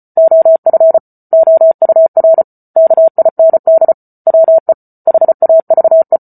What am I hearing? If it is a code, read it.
Morse code, heard as a keyed tone, about 34 wpm, 650 Hz. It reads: OF OUR KIND WE HAVE
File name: Book Two / 08-Chapter 8 - 34 WPM